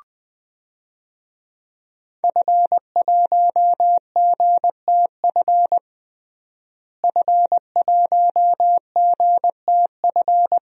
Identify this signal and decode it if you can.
Morse code, heard as a keyed tone, about 20 wpm, 700 Hz. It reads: F1GTF F1GTF
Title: DXCC-100 Call Signs - 5 Characters - Repeated 2x - 20wpm